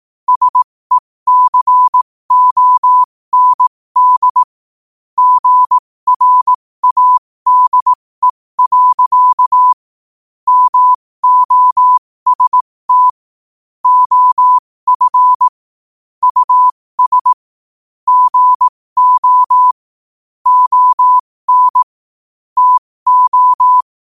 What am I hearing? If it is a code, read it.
Morse code, heard as a keyed tone, about 18 wpm, spaced as for 15 wpm, 1000 Hz. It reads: SECOND GRADE. MOST OF US GO ON TO
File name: qotdfiles / 15wpm